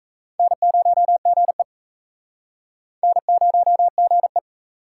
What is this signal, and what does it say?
Morse code, heard as a keyed tone, about 35 wpm, 700 Hz. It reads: N0GE N0GE
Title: Speed Racing - USA 1x2 Call Signs 28wpm